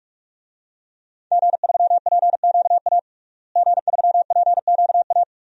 Morse code, heard as a keyed tone, about 42 wpm, 700 Hz. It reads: G3PQA G3PQA
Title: DXCC-100 Call Signs - 5 Characters - Repeated 2x - 45wpm